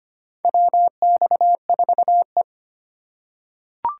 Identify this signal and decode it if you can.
Morse code, heard as a keyed tone, about 25 wpm, 700 Hz. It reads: WX4E
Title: USA Call Signs - 4 Characters - 4x Character Spacing - 25wpm